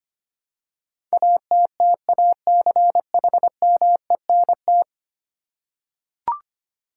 Morse code, heard as a keyed tone, about 25 wpm, 700 Hz. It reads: ATTACHMENT
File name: Ten Letter Words - 25wpm